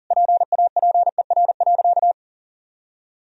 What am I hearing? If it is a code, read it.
Morse code, heard as a keyed tone, about 40 wpm, 700 Hz. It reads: PAPER.
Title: Sentences from Top 1000 Words - Review Part 1 - 40wpm